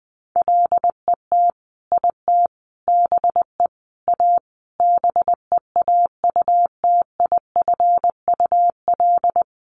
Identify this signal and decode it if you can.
Morse code, heard as a keyed tone, about 20 wpm, 700 Hz. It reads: LET IT BE A BEAUTIFUL
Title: Sentences from Top 1100 Words 20wpm